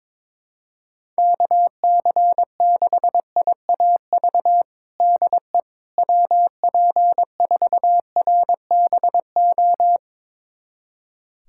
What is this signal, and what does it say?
Morse code, heard as a keyed tone, about 22 wpm, 700 Hz. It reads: KC6IAV DE WP4RBO
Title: USA Callsign DE Callsign 22wpm